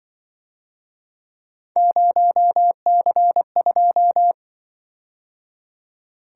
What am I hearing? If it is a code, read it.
Morse code, heard as a keyed tone, about 25 wpm, 700 Hz. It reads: 0C2
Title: Speed-Racing - Three Letter-Number Combinations - 20wpm